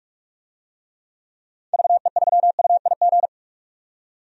Morse code, heard as a keyed tone, about 46 wpm, 700 Hz. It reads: VE3UIG
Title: DXCC-100 Call Signs - 6 Characters - 45wpm